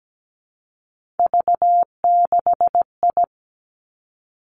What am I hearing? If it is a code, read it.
Morse code, heard as a keyed tone, about 17 wpm, 700 Hz. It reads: V6I
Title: Three Challenging Letter-Number Combinations 17wpm